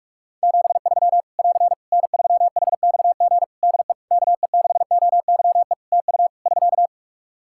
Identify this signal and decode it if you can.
Morse code, heard as a keyed tone, about 42 wpm, 700 Hz. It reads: 73 <AR> N3HXG DE KE6OYE TU <SK>
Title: Practice QSO - Exchange 4 - Ending - 45wpm